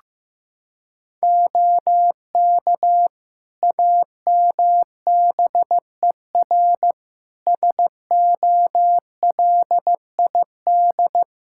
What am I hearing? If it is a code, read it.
Morse code, heard as a keyed tone, about 15 wpm, 700 Hz. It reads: OK AMBER SOLID
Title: Practice QSO - Exchange 3 - More Info - 15wpm